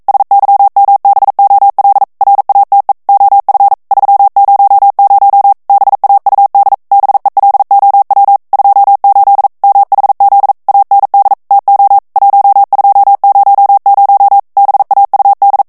Morse code, heard as a keyed tone, about 42 wpm, 800 Hz. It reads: SYMBOL RATE OF 300 BAUD BELOW 28 MHZ AND TO 1200 BAUD